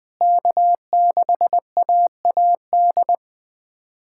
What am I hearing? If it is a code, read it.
Morse code, heard as a keyed tone, about 20 wpm, 690 Hz. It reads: K6AAD
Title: SOTA QSO Practice 20wpm